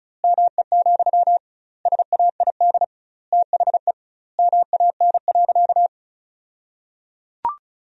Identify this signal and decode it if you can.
Morse code, heard as a keyed tone, about 36 wpm, 700 Hz. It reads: ME, SAID THE MAN.
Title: 2x Word-Spacing - Sentences from Top 200 Words - 35wpm